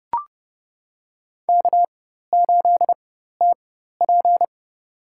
Morse code, heard as a keyed tone, about 30 wpm, spaced as for 14 wpm, 700 Hz. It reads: K8TP
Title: USA Call Signs - 4 Characters - 4x Character Spacing - 30wpm